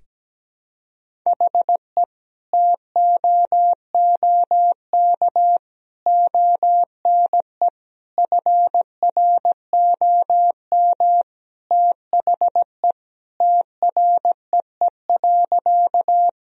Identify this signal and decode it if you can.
Morse code, heard as a keyed tone, about 17 wpm, 700 Hz. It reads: HE TOOK ONE FROM THE TREE.
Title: Sentences from Top 1000 Words - Review Part 1 - 17wpm